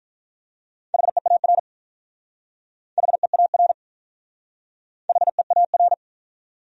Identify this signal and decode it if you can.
Morse code, heard as a keyed tone, about 43 wpm, 700 Hz. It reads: HEAR HEAR HEAR
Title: Speed-Racing Top 500 Words 35wpm